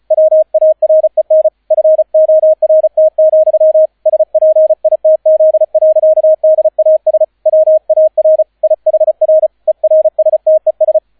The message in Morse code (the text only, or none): WAREN FORT, SPITZ. DAS WAR IHR ERSTES